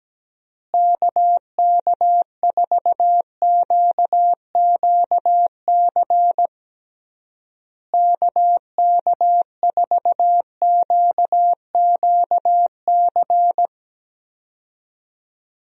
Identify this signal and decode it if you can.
Morse code, heard as a keyed tone, about 17 wpm, 700 Hz. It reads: KK4QQC KK4QQC
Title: USA Call Signs - 6 Characters - Repeated 3x - 17wpm